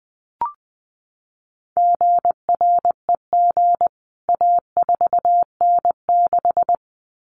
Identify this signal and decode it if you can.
Morse code, heard as a keyed tone, about 20 wpm, 700 Hz. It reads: GREG A4N6
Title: CWT Exchange 20wpm